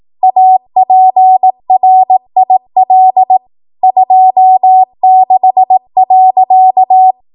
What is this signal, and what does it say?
Morse code, heard as a keyed tone, about 18 wpm, 750 Hz. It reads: APRIL 26.
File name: Morse / 18WPM